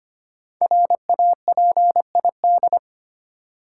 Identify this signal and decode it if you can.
Morse code, heard as a keyed tone, about 25 wpm, 700 Hz. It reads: RAPID
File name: 5 Letter Words - First 3 Letters Given - 25wpm